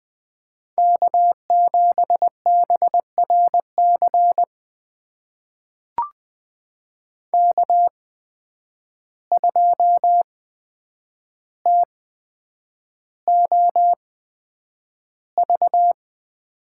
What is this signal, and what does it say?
Morse code, heard as a keyed tone, about 20 wpm, spaced as for 9 wpm, 700 Hz. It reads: K7BRC K2TOV
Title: USA Call Signs - 5 Characters - 8x Character Spacing - 20wpm